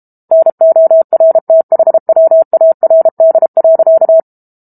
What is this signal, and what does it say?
Morse code, heard as a keyed tone, about 32 wpm, 650 Hz. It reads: NORTHWARD.
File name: Book Two / 06-Chapter 6 - 32 WPM